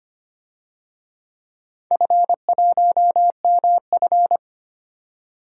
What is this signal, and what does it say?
Morse code, heard as a keyed tone, about 25 wpm, 700 Hz. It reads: F1MF
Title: DXCC-100 Call Signs - 4 Characters - 25wpm